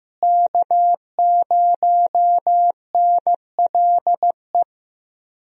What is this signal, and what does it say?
Morse code, heard as a keyed tone, about 15 wpm, 700 Hz. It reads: K0NLE
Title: USA Call Signs - 5 Characters - Repeated 3x - 15wpm